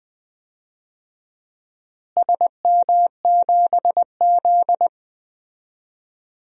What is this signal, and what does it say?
Morse code, heard as a keyed tone, about 20 wpm, 700 Hz. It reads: SM7Z
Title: DXCC-100 Call Signs - 4 Characters - Repeated 2x - 20wpm